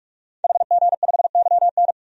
Morse code, heard as a keyed tone, about 44 wpm, 690 Hz. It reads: HG5YN
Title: DXCC-100 Call Signs - 5 Characters - 45wpm